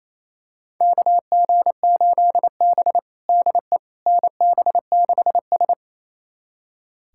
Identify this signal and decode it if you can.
Morse code, heard as a keyed tone, about 28 wpm, 700 Hz. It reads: KG8B DE NB6S
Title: USA Callsign DE Callsign - 4 Characters - 28wpm